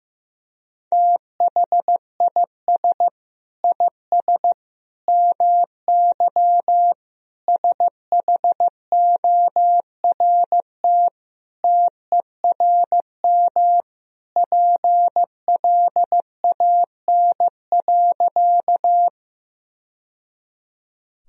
Morse code, heard as a keyed tone, about 15 wpm, 700 Hz. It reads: THIS IS MY SHORT TERM PLAN.
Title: Sentences from Top 1000 Words - Review Part 4 - 15wpm